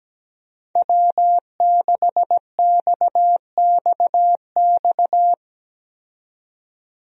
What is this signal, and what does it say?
Morse code, heard as a keyed tone, about 17 wpm, 700 Hz. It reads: W6XXX
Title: USA Call Signs - 5 Characters - 17wpm